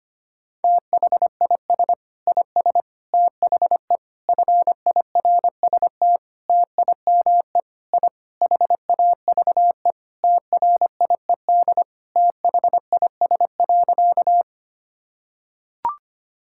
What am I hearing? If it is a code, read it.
Morse code, heard as a keyed tone, about 25 wpm, 700 Hz. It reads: THIS IS THE FIRST TIME I HAVE TRIED THIS.
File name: practice / 25wpm / Sentences with 4-Word Phrases - 25wpm